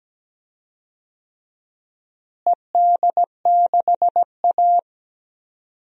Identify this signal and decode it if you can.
Morse code, heard as a keyed tone, about 17 wpm, 700 Hz. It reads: ED6A
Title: DXCC-100 Call Signs - 4 Characters - Repeated 2x - 17wpm